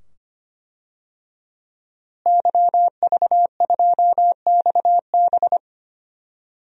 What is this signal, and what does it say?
Morse code, heard as a keyed tone, about 25 wpm, 700 Hz. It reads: YV2XB
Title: DXCC-100 Call Signs - 5 Characters - Repeated 2x - 25wpm